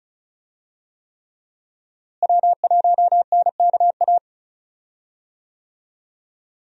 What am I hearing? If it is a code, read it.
Morse code, heard as a keyed tone, about 34 wpm, 700 Hz. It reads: W1NKA
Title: USA Call Signs - 5 Characters - Repeated 3x - 35wpm